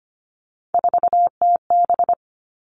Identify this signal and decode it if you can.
Morse code, heard as a keyed tone, about 25 wpm, 700 Hz. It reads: 4TB
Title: All Cut Numbers - 3-Digit 25wpm